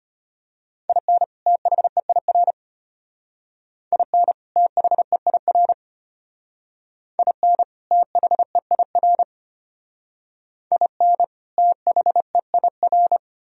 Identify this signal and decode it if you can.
Morse code, heard as a keyed tone, about 32 wpm, 700 Hz. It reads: IN THEIR IN THEIR IN THEIR IN THEIR